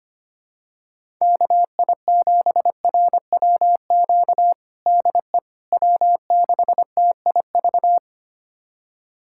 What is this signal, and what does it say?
Morse code, heard as a keyed tone, about 25 wpm, 700 Hz. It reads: KI7RWQ DE W6TIV
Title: USA Callsign DE Callsign 25wpm